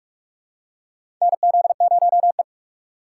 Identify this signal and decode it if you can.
Morse code, heard as a keyed tone, about 46 wpm, 700 Hz. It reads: NZ0E
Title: USA Call Signs - 4 Characters - 4x Character Spacing - 45wpm